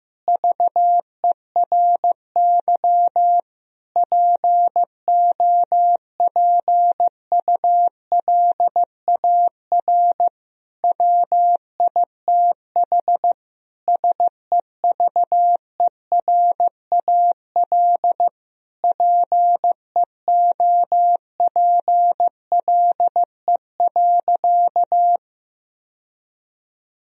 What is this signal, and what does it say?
Morse code, heard as a keyed tone, about 16 wpm, 700 Hz. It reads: VERY POPULAR WITH SEVERAL PEOPLE.
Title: Sentences from Top 1000 Words - Review Part 3 - 15wpm